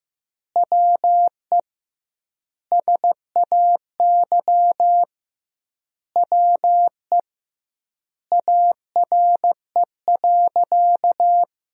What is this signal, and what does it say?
Morse code, heard as a keyed tone, about 15 wpm, 700 Hz. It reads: WE SAY WE ARE.
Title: 2x Word-Spacing - Sentences from Top 200 Words - 15wpm